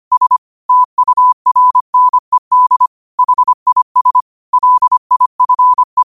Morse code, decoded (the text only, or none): S TURNED HIS LIFE